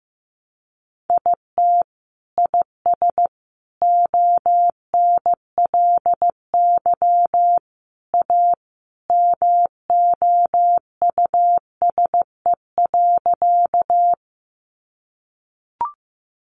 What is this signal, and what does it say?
Morse code, heard as a keyed tone, about 15 wpm, 700 Hz. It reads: IT IS ONLY A MOUSE.
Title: Sentences from Top 700 Words 15wpm